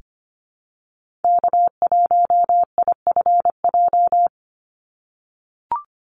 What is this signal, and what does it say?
Morse code, heard as a keyed tone, about 25 wpm, 700 Hz. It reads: K1IFJ
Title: Call Signs - 25wpm